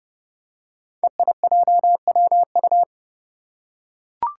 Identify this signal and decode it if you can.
Morse code, heard as a keyed tone, about 29 wpm, 700 Hz. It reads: EIJWU
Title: Five Challenging Letter-Number Combinations 30wpm